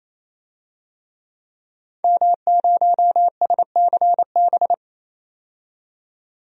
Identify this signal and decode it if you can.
Morse code, heard as a keyed tone, about 28 wpm, 700 Hz. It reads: M0SCB
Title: DXCC-100 Call Signs - 5 Characters - 28wpm